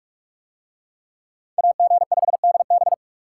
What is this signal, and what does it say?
Morse code, heard as a keyed tone, about 44 wpm, 700 Hz. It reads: AG5DB